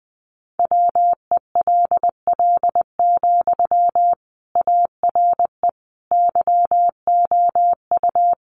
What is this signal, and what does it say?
Morse code, heard as a keyed tone, about 20 wpm, 700 Hz. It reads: WELL, ARE YOU
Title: Sentences from Top 400 Words 20wpm